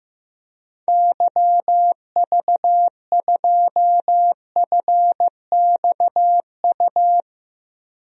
Morse code, heard as a keyed tone, about 15 wpm, 700 Hz. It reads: YV2FXU
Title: DXCC-100 Call Signs - 6 Characters - Repeated 2x - 15wpm